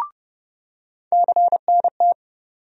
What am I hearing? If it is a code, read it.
Morse code, heard as a keyed tone, about 30 wpm, 700 Hz. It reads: CNT
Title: USA Call Sign Postfixes - 3 Letters 30wpm